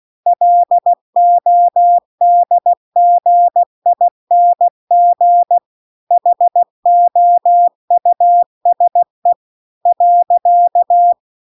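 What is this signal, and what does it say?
Morse code, heard as a keyed tone, about 16 wpm, 700 Hz. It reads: LODGING HOUSE .
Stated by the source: Morse practice text at 16 WPM.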